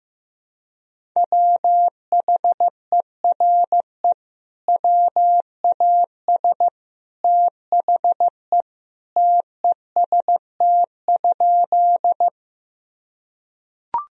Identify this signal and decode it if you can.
Morse code, heard as a keyed tone, about 15 wpm, 700 Hz. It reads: WHERE WAS THE TEST?
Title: Sentences from Top 1000 Words - Review Part 3 - 15wpm